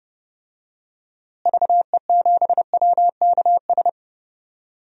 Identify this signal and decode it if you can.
Morse code, heard as a keyed tone, about 32 wpm, 700 Hz. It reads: VE7WKS
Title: DXCC-100 Call Signs - 6 Characters - Repeated 3x - 30wpm